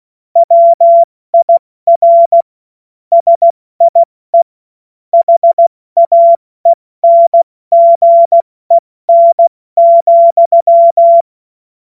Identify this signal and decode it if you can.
Morse code, heard as a keyed tone, about 16 wpm, 680 Hz. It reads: WIR SIE HAENGEN,
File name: Morsekurs / Witze / Witz 09. Dezember 2025 70 F80